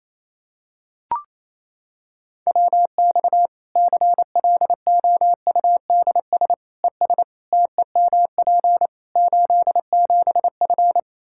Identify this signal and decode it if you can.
Morse code, heard as a keyed tone, about 28 wpm, 700 Hz. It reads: WX CLOUDS ES TEMP 87F
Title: Practice QSO - Exchange 2 - Info - 28wpm